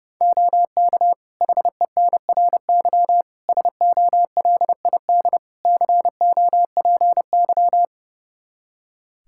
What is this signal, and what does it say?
Morse code, heard as a keyed tone, about 30 wpm, 700 Hz. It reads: OK HENRY SOLID COPY
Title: Practice QSO - Exchange 3 - More Info - 30wpm